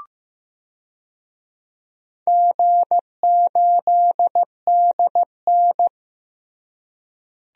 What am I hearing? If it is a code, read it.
Morse code, heard as a keyed tone, about 15 wpm, 700 Hz. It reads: G8DN
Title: DXCC-100 Call Signs - 4 Characters - Repeated 3x - 15wpm